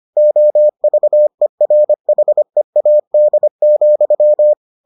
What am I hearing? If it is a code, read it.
Morse code, heard as a keyed tone, about 25 wpm, 600 Hz. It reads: OVERHEAD,